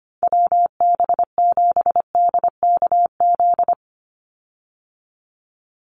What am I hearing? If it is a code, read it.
Morse code, heard as a keyed tone, about 25 wpm, 700 Hz. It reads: WB7DKZ
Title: Call Signs Repeated 3x - 25wpm